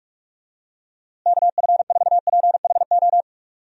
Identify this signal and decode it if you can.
Morse code, heard as a keyed tone, about 42 wpm, 700 Hz. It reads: KF4PHO
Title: USA Call Signs - 6 Characters - Repeated 3x - 45wpm